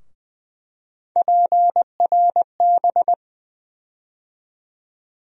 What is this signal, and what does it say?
Morse code, heard as a keyed tone, about 20 wpm, 700 Hz. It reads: PRB
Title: USA Call Sign Postfixes - 3 Letters 20wpm